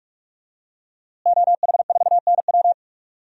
Morse code, heard as a keyed tone, about 45 wpm, 700 Hz. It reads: OH4NW
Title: DXCC-100 Call Signs - 5 Characters - Repeated 3x - 45wpm